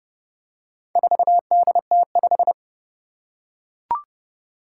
Morse code, heard as a keyed tone, about 30 wpm, 700 Hz. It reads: 4DT5
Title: Four Challenging Letter-Number Combinations 30wpm